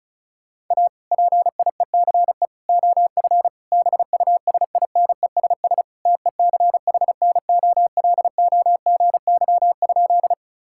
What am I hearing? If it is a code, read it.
Morse code, heard as a keyed tone, about 34 wpm, 690 Hz. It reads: A PIECE OF BUSINESS TECHNOLOGY?
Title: Sentences from Top 1000 Words - Review Part 5 - 35wpm